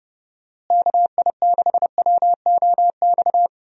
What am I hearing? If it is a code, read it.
Morse code, heard as a keyed tone, about 30 wpm, 700 Hz. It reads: KI6WOX
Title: USA Call Signs - 6 Characters - Repeated 2x - 30wpm